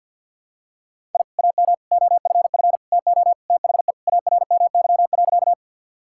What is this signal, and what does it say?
Morse code, heard as a keyed tone, about 48 wpm, 700 Hz. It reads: I AM OFF TO THE ARMY.
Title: Sentences from Top 1000 Words - Review Part 5 - 50wpm